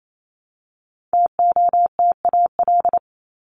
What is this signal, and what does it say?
Morse code, heard as a keyed tone, about 28 wpm, 700 Hz. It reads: TOTAL
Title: LY Suffix - Top 100 Words with Base Words 28wpm